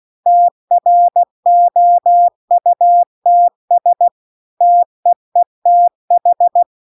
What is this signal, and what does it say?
Morse code, heard as a keyed tone, about 16 wpm, 700 Hz. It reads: TROUTS TEETH
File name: CW-The-Big-Twoheaded-River-2-16WPM